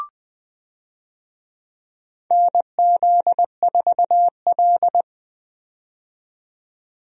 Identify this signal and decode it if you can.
Morse code, heard as a keyed tone, about 21 wpm, 700 Hz. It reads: NZ4L
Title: USA Call Signs - 4 Characters - 20wpm